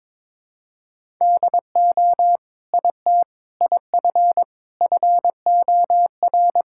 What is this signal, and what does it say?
Morse code, heard as a keyed tone, about 21 wpm, 700 Hz. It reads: DO IT IF FOR
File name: Sentences with 7-Word Phrases - 22wpm